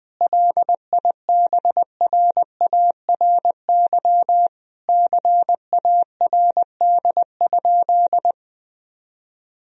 Morse code, heard as a keyed tone, about 20 wpm, 700 Hz. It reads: LIBRARY CARD?
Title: Sentences from Top 1000 Words - Review Part 4 - 20wpm